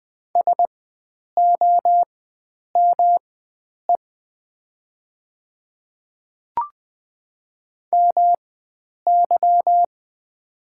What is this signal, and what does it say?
Morse code, heard as a keyed tone, about 20 wpm, spaced as for 9 wpm, 700 Hz. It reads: SOME MY
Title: 4x Character Spacing - Top 100 Words - 20wpm